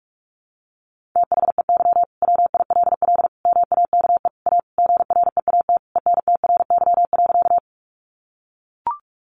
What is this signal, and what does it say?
Morse code, heard as a keyed tone, about 44 wpm, 700 Hz. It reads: THEY WILL MAKE A GREAT ENTRY.
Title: Sentences from Top 1500 Words 45wpm